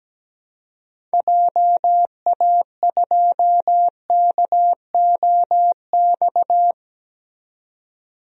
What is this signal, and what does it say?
Morse code, heard as a keyed tone, about 17 wpm, 700 Hz. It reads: JA2KOX
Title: DXCC-100 Call Signs - 6 Characters - 17wpm